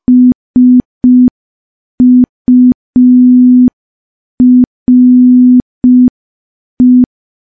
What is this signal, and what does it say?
Morse code, heard as a keyed tone, about 5 wpm, 250 Hz. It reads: SURE
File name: morse / morse-training-05WPM